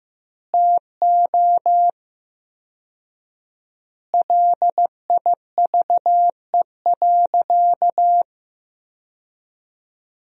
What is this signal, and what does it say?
Morse code, heard as a keyed tone, about 15 wpm, 700 Hz. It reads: TO LIVE.
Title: 4x Word-Spacing - Sentences from Top 300 Words - 15wpm